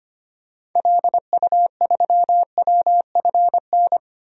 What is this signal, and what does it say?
Morse code, heard as a keyed tone, about 25 wpm, 700 Hz. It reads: LU3WFN
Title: DXCC-100 Call Signs - 6 Characters - 25wpm